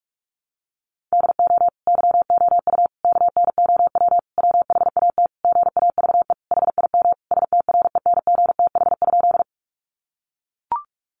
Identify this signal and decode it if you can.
Morse code, heard as a keyed tone, about 44 wpm, 700 Hz. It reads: DO YOU KNOW WHAT GAVE HIM STRENGTH?
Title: Sentences from Top 1300 Words 45wpm